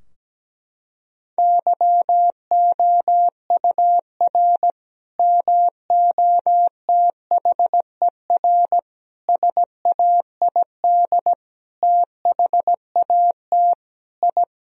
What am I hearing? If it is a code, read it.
Morse code, heard as a keyed tone, about 17 wpm, 700 Hz. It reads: YOUR MOTHER SAID THAT I
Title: Sentences from Top 1000 Words - Review Part 2 - 17wpm